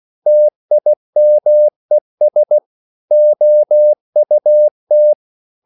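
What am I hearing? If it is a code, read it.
Morse code, heard as a keyed tone, about 16 wpm, 600 Hz. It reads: TIMES OUT
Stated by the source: English quotations sent in CW as practice at 16 words per minute.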